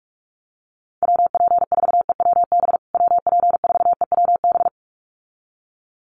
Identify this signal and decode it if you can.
Morse code, heard as a keyed tone, about 44 wpm, 700 Hz. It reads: WP4EWB WP4EWB
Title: USA Call Signs Repeated 2x 45wpm